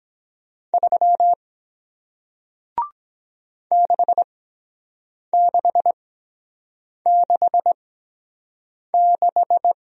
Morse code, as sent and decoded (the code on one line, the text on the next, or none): ...-- / -.... -.... -.... -....
3 6666